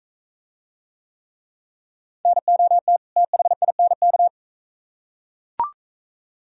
Morse code, heard as a keyed tone, about 42 wpm, 700 Hz. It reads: NOT THINK